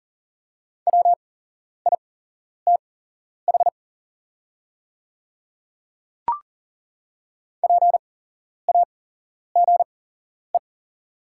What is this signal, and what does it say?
Morse code, heard as a keyed tone, about 39 wpm, spaced as for 11 wpm, 700 Hz. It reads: WITH PAGE
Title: 8x Character Spacing - Top 500 Words - 40wpm